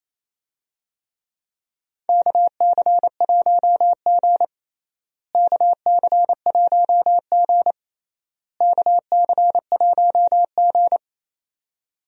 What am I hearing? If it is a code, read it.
Morse code, heard as a keyed tone, about 28 wpm, 700 Hz. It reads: KC1G KC1G KC1G
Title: USA Call Signs - 4 Characters - Repeated 3x - 28wpm